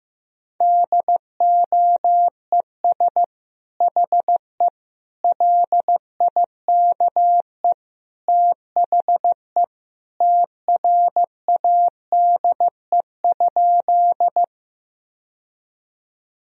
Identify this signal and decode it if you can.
Morse code, heard as a keyed tone, about 15 wpm, 700 Hz. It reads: DOES HE LIKE THE TRADE?
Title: Sentences from Top 1000 Words - Review Part 4 - 15wpm